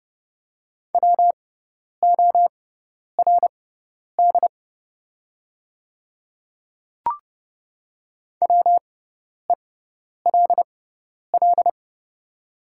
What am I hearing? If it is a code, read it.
Morse code, heard as a keyed tone, about 30 wpm, spaced as for 10 wpm, 690 Hz. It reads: WORD WELL
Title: 6x Character Spacing - Top 200 Words - 30wpm